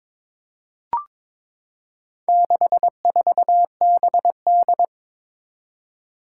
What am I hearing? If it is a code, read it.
Morse code, heard as a keyed tone, about 22 wpm, 700 Hz. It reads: 64BD
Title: Four Challenging Letter-Number Combinations 22wpm